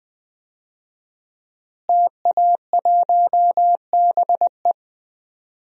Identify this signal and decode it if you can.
Morse code, heard as a keyed tone, about 21 wpm, 700 Hz. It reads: TA1BE
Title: DXCC-100 Call Signs - 5 Characters - Repeated 3x - 20wpm